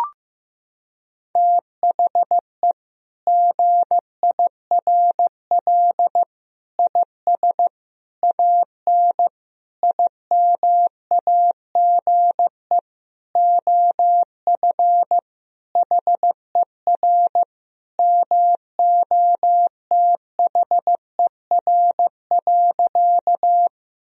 Morse code, heard as a keyed tone, about 15 wpm, 700 Hz. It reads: THE GIRL IS AN IMAGE OF HER MOTHER.
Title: Sentences from Top 1000 Words - Review Part 4 - 15wpm